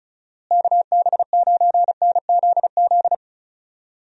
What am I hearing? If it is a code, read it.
Morse code, heard as a keyed tone, about 34 wpm, 700 Hz. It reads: KB9NZZ
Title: USA Call Signs - 6 Characters - Repeated 2x - 35wpm